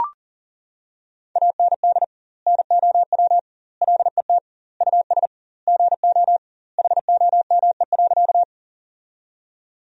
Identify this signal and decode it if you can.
Morse code, heard as a keyed tone, about 40 wpm, 700 Hz. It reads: AND NOW LET US GO HOME.
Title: 2x Word-Spacing - Sentences from Top 300 Words - 40wpm